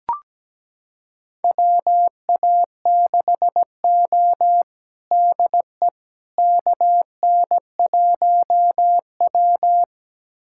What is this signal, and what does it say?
Morse code, heard as a keyed tone, about 17 wpm, 700 Hz. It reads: WA6O DE KN1W
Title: USA Callsign DE Callsign - 4 Characters - 17wpm